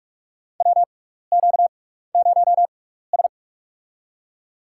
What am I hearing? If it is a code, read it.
Morse code, heard as a keyed tone, about 46 wpm, spaced as for 16 wpm, 700 Hz. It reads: WQ0S
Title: USA Call Signs - 4 Characters - 6x Character Spacing - 45wpm